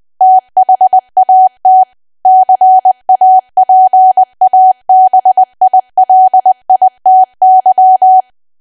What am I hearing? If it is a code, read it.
Morse code, heard as a keyed tone, about 20 wpm, 750 Hz. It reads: THAT CAPABILITY
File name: Archive / 20 WPM / 20WPM